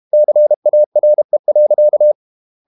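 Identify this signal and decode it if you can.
Morse code, heard as a keyed tone, about 32 wpm, 600 Hz. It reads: CARE.
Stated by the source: Morse code practice at 32 wpm from CW med Gnister, reading Ken Follett's novel Never in English.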